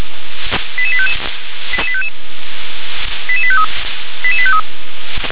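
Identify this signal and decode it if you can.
Five-tone selective call (ZVEI), heard as quick runs of five tones, tones of about 70 ms. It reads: 88855 88855 88853 88853
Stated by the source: ZVEI five-tone call received by SDR.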